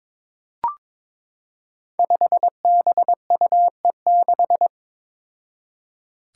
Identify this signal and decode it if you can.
Morse code, heard as a keyed tone, about 22 wpm, 700 Hz. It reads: 5BUE6
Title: Five Challenging Letter-Number Combinations 22wpm